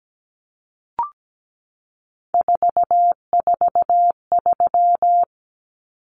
Morse code, heard as a keyed tone, about 17 wpm, 700 Hz. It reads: 443
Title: Common Cut Numbers - 3-Digit 17wpm